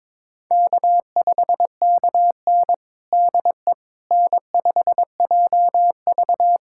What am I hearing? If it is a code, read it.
Morse code, heard as a keyed tone, about 22 wpm, 700 Hz. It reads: K5KN DE N5JV
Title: USA Callsign DE Callsign - 4 Characters - 22wpm